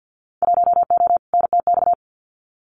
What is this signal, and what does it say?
Morse code, heard as a keyed tone, about 50 wpm, 700 Hz. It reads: 1O NTX
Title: Winter Field Day Exchanges 50wpm